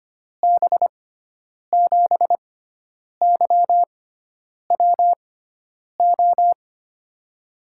Morse code, heard as a keyed tone, about 25 wpm, spaced as for 9 wpm, 700 Hz. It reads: B7YWO
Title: USA Call Signs - 6 Characters - 6x Character Spacing - 25wpm